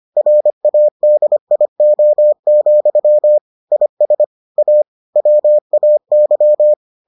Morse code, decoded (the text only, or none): RADIO, IS A WAY